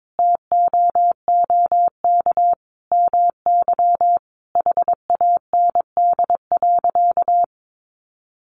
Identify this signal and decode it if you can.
Morse code, heard as a keyed tone, about 22 wpm, 700 Hz. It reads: TOOK MY HAND.
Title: Sentences from Top 1100 Words 22wpm